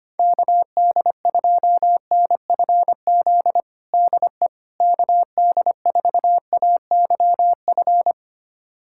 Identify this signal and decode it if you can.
Morse code, heard as a keyed tone, about 25 wpm, 700 Hz. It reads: KD2NFZ DE KD4AYF
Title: USA Callsign DE Callsign - 6 Characters - 25wpm